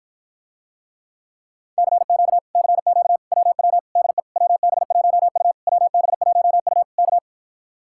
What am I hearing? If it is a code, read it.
Morse code, heard as a keyed tone, about 54 wpm, 700 Hz. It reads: CQ CQ WW DE WB1U WB1U K